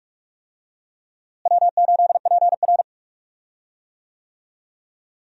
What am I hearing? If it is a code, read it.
Morse code, heard as a keyed tone, about 42 wpm, 700 Hz. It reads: W8PR
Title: USA Call Signs - 4 Characters - Repeated 2x - 45wpm